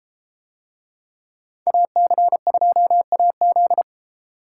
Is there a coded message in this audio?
Yes: Morse code from 1.7 to 3.8 s, heard as a keyed tone, about 33 wpm, 700 Hz. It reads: AC2AZ